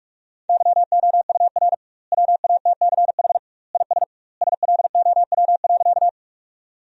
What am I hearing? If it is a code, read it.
Morse code, heard as a keyed tone, about 46 wpm, 700 Hz. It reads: YOUR WATCH IS SLOW.
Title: 2x Word-Spacing - Sentences from Top 400 Words - 45wpm